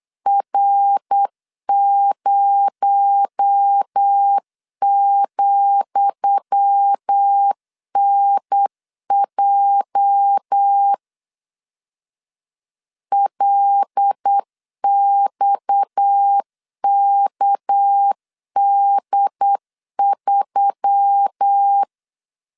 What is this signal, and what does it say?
Morse code, heard as a keyed tone, about 8 wpm, 800 Hz. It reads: R0,NJ LXKD3
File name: LEZIONI / 10 SIGNS / 10D COMA 8wpm 3749